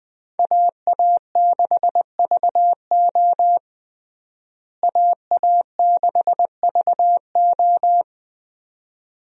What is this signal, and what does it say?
Morse code, heard as a keyed tone, about 20 wpm, 700 Hz. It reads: AA6VO AA6VO
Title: USA Call Signs - 5 Characters - Repeated 2x - 20wpm